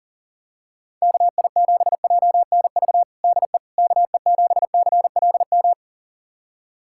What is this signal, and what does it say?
Morse code, heard as a keyed tone, about 40 wpm, 700 Hz. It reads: KI7JNV DE KE7CLM
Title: USA Callsign DE Callsign - 6 Characters - 40wpm